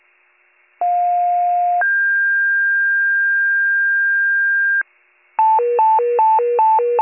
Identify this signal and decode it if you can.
Two-tone page, heard as a one-second tone, then a three-second tone, about 710 then 1600 Hz, with noise. It is heard from 0.8 to 4.8 s.